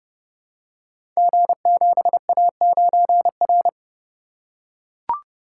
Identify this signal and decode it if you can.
Morse code, heard as a keyed tone, about 30 wpm, 700 Hz. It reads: G7A9R
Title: Code Groups - 5 Letter:Numbers - 30wpm